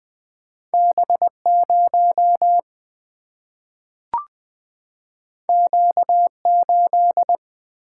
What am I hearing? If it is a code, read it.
Morse code, heard as a keyed tone, about 20 wpm, 700 Hz. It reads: B0 Q8
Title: Single-Letter Single-Number 20wpm